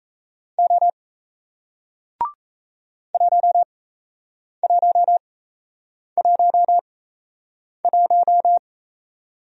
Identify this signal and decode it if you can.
Morse code, heard as a keyed tone, about 38 wpm, spaced as for 8 wpm, 700 Hz. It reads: O 1111